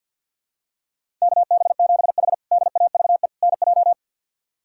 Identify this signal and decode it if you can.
Morse code, heard as a keyed tone, about 53 wpm, 700 Hz. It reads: KB7H DAVE NJ
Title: K1USN SST QSO Practice - 50wpm